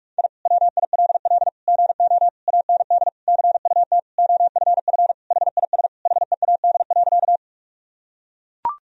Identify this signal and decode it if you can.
Morse code, heard as a keyed tone, about 41 wpm, 700 Hz. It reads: I WILL GO AND CUT OFF HIS HEAD.
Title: Sentences from Top 1000 Words - Review Part 1 - 45wpm